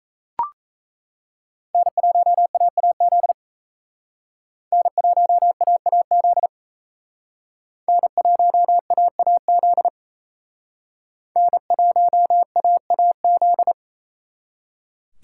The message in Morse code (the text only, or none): N1AAZ N1AAZ N1AAZ N1AAZ